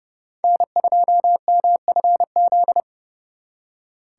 Morse code, heard as a keyed tone, about 36 wpm, 700 Hz. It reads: N2MFZ
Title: Speed Racing - USA 1x3 Call Signs 30wpm